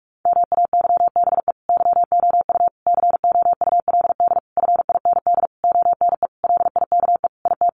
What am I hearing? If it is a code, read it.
Morse code, heard as a keyed tone, about 44 wpm, 700 Hz. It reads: MAYBE YOU COULD FIND ONE LIKE IT
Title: Sentences from Top 1400 Words 45wpm